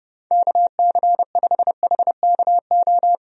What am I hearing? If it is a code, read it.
Morse code, heard as a keyed tone, about 30 wpm, 700 Hz. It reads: KC5HKO
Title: USA Call Signs - 6 Characters - 30wpm